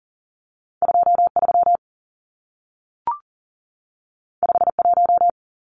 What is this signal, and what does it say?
Morse code, heard as a keyed tone, about 40 wpm, 700 Hz. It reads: 23 51
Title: Numbers - 2-Digit 40wpm